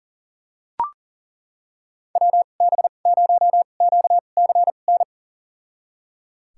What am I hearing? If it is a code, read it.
Morse code, heard as a keyed tone, about 38 wpm, spaced as for 28 wpm, 700 Hz. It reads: WB0QCN
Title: USA Call Signs - 6 Characters - 2x Character Spacing - 40wpm